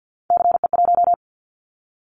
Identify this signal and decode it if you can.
Morse code, heard as a keyed tone, about 48 wpm, 700 Hz. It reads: CE1
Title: Callsign Prefixes for Top 100 DXCC - 50wpm